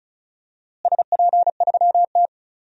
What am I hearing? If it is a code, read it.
Morse code, heard as a keyed tone, about 36 wpm, 700 Hz. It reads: SP3T
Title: DXCC-100 Call Signs - 4 Characters - Repeated 3x - 35wpm